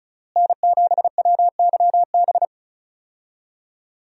Morse code, heard as a keyed tone, about 36 wpm, 700 Hz. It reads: N7WYB
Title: USA Call Signs - 5 Characters - Repeated 3x - 35wpm